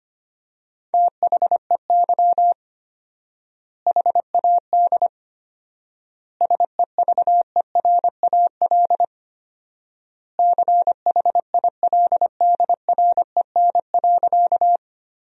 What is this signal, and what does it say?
Morse code, heard as a keyed tone, about 25 wpm, 700 Hz. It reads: THEY HAD SEVERAL CHILDREN.
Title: 4x Word-Spacing - Sentences from Top 400 Words - 25wpm